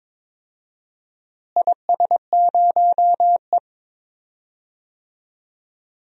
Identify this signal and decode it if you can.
Morse code, heard as a keyed tone, about 21 wpm, 700 Hz. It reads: IS0E
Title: DXCC-100 Call Signs - 4 Characters - Repeated 2x - 22wpm